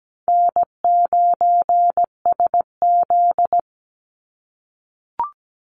Morse code, heard as a keyed tone, about 17 wpm, 700 Hz. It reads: N9SZ
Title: Easy Call Signs - 17wpm